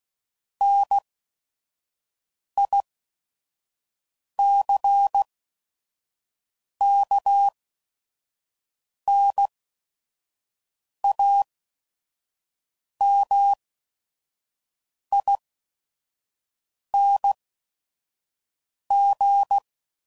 Morse code, heard as a keyed tone, about 16 wpm, spaced as for 5 wpm, 800 Hz. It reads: NICKNAMING
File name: CW Code Groups @5WPM / Technical Subjects / OR Gate